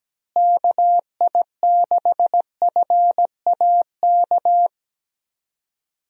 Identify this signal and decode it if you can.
Morse code, heard as a keyed tone, about 17 wpm, 700 Hz. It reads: KI6FAK
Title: USA Call Signs - 6 Characters - Repeated 3x - 17wpm